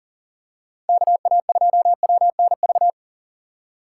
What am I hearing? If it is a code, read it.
Morse code, heard as a keyed tone, about 37 wpm, 700 Hz. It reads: KA2WNV
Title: USA Call Signs - 6 Characters - 6x Character Spacing - 40wpm